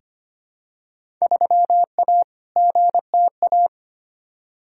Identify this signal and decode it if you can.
Morse code, heard as a keyed tone, about 25 wpm, 700 Hz. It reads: 3A GTA
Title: Field Day Exchanges - 25wpm